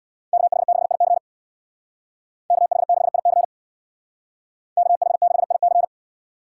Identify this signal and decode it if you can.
Morse code, heard as a keyed tone, about 62 wpm, 700 Hz. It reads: KH6IC KH6IC KH6IC
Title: Speed Racing - USA 2x2 Call Signs 50wpm